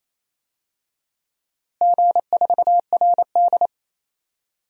Morse code, heard as a keyed tone, about 27 wpm, 700 Hz. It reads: G4RD
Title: DXCC-100 Call Signs - 4 Characters - Repeated 3x - 28wpm